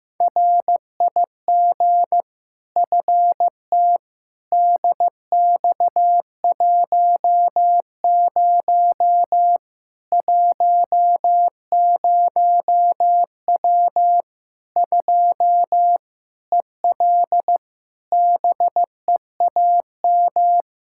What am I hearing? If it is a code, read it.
Morse code, heard as a keyed tone, about 15 wpm, 700 Hz. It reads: RIG FT DX10 10W 2 EL BEAM
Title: K1USN SST QSO Practice 15wpm